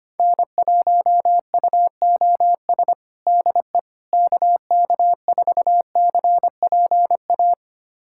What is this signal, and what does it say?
Morse code, heard as a keyed tone, about 25 wpm, 700 Hz. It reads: N1UOS DE KK4CPA
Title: USA Callsign DE Callsign 25wpm